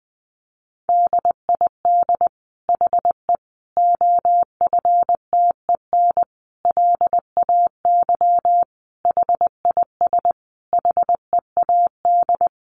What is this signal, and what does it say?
Morse code, heard as a keyed tone, about 20 wpm, 700 Hz. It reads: DID HE OFTEN LAY HIS HEAD
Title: Sentences from Top 1000 Words 20wpm